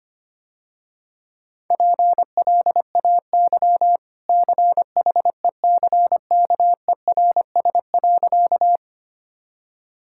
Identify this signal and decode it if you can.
Morse code, heard as a keyed tone, about 25 wpm, 700 Hz. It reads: PLAY CHECKERS.
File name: Easy Sentences with Hints - 25wpm